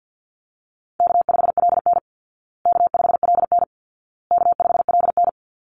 Morse code, heard as a keyed tone, about 50 wpm, 700 Hz. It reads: K5LN K5LN K5LN
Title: Easy Call Signs Repeated 3x - 50wpm